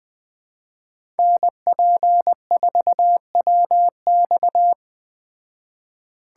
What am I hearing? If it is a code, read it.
Morse code, heard as a keyed tone, about 20 wpm, 700 Hz. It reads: NP4WX
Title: Speed Racing - USA 2x2 Call Signs 17wpm